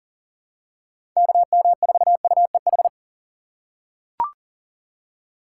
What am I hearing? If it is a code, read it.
Morse code, heard as a keyed tone, about 40 wpm, 700 Hz. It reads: KM4UEH
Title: USA Call Signs - 6 Characters - 3x Character Spacing - 40wpm